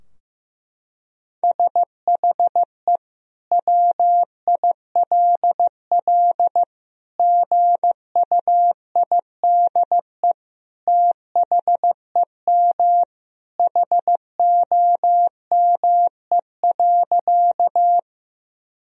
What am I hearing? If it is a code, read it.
Morse code, heard as a keyed tone, about 16 wpm, 700 Hz. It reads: SHE WILL GUIDE THEM HOME.
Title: Sentences from Top 1000 Words - Review Part 3 - 15wpm